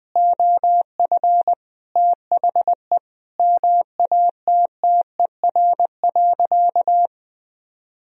Morse code, heard as a keyed tone, about 20 wpm, 700 Hz. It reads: OF THE MATTER.
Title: Sentences from Top 1000 Words - Review Part 3 - 20wpm